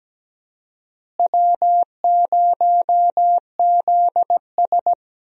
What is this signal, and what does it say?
Morse code, heard as a keyed tone, about 17 wpm, 700 Hz. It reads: W0ZS
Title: USA Call Signs - 4 Characters - Repeated 3x - 17wpm